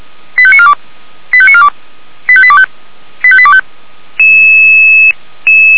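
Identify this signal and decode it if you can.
Five-tone selective call (ZVEI), heard as quick runs of five tones, tones of about 70 ms. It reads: 85942 85942 85825 85825
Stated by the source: ZVEI five-tone selective call received on an SDR.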